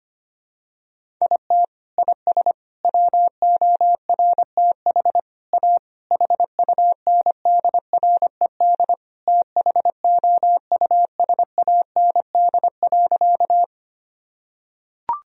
Morse code, heard as a keyed tone, about 25 wpm, 700 Hz. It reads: IT IS WORTH A HUNDRED THOUSAND.
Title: Sentences from Top 1000 Words - Review Part 3 - 25wpm